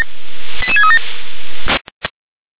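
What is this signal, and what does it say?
Five-tone selective call (ZVEI), heard as quick runs of five tones, tones of about 70 ms. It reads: 88637
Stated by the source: ZVEI five-tone selective call 88730 received on an SDR.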